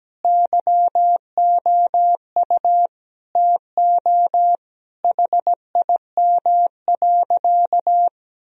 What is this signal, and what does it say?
Morse code, heard as a keyed tone, about 17 wpm, 700 Hz. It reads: YOU TO HIM.
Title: Sentences from Top 1000 Words - Review Part 3 - 17wpm